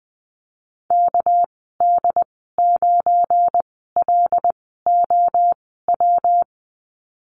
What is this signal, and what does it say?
Morse code, heard as a keyed tone, about 20 wpm, spaced as for 14 wpm, 700 Hz. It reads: KD9LOW
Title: 231 Lesson 37g - Letter x - Call Signs 20wpm